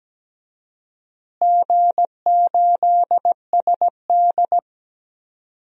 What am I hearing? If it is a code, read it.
Morse code, heard as a keyed tone, about 17 wpm, 700 Hz. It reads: G8SD
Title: DXCC-100 Call Signs - 4 Characters - Repeated 2x - 17wpm